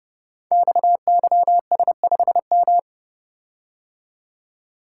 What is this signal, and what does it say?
Morse code, heard as a keyed tone, about 30 wpm, 700 Hz. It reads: XYS5M
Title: Code Groups - 5 Letter:Numbers - 30wpm